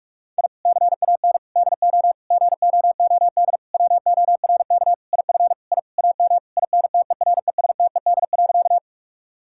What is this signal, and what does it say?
Morse code, heard as a keyed tone, about 44 wpm, 700 Hz. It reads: I CAN DO GOOD WORK IF I AM INTERESTED.
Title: Sentences from Top 1000 Words - Review Part 6 - 45wpm